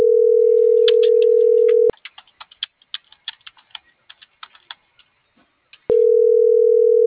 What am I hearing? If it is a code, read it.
Ringback tone (North America), 440 + 480 Hz.